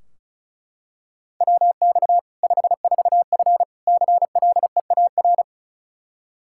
Morse code, heard as a keyed tone, about 36 wpm, 700 Hz. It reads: WX 54F CLEAR
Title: K1USN SST QSO Practice 35wpm